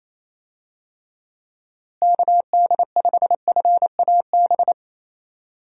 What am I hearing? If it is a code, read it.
Morse code, heard as a keyed tone, about 27 wpm, 700 Hz. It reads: KD5FAB
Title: USA Call Signs - 6 Characters - Repeated 2x - 28wpm